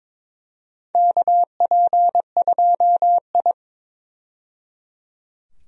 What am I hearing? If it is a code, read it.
Morse code, heard as a keyed tone, about 22 wpm, 700 Hz. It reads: KP2I
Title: DXCC-100 Call Signs - 4 Characters - 22wpm